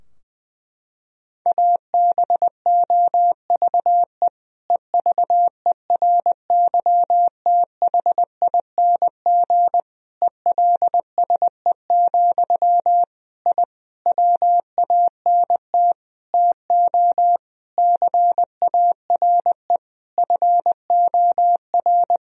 Morse code, heard as a keyed tone, about 20 wpm, 700 Hz. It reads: ABOVE EVERYTHING ELSE, I WANT TO CARE FOR
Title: Sentences from Top 1000 Words - Review Part 2 - 20wpm